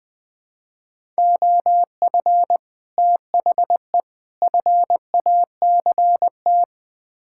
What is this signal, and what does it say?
Morse code, heard as a keyed tone, about 21 wpm, 700 Hz. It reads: OF THE FACT
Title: Build A Phrase - 4-Word Phrases - 20wpm